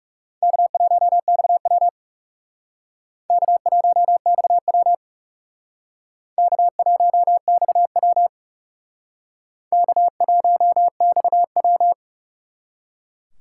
Morse code, heard as a keyed tone, about 37 wpm, 700 Hz. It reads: K1XW K1XW K1XW K1XW
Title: Speed Racing - USA 1x2 Call Signs 30wpm